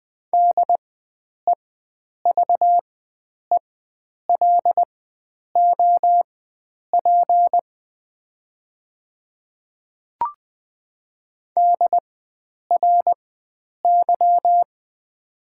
Morse code, heard as a keyed tone, about 20 wpm, spaced as for 9 wpm, 700 Hz. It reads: DEVELOP DRY
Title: 4x Character Spacing - Top 500 Words - 20wpm